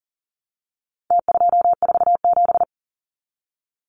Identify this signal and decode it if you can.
Morse code, heard as a keyed tone, about 40 wpm, 700 Hz. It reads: T247
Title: Numbers - Contest Style - 4-Digit 40wpm